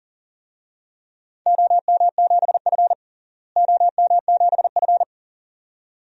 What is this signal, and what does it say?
Morse code, heard as a keyed tone, about 37 wpm, 700 Hz. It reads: OM7F OM7F
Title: DXCC-100 Call Signs - 4 Characters - Repeated 2x - 40wpm